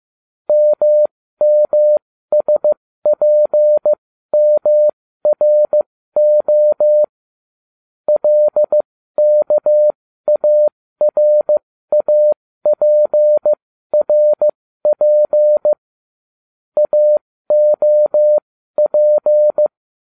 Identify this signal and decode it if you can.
Morse code, heard as a keyed tone, about 15 wpm, spaced as for 13 wpm, 600 Hz. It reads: MMSPMRO LKARAPRP AOP
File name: cw / ubacw09 1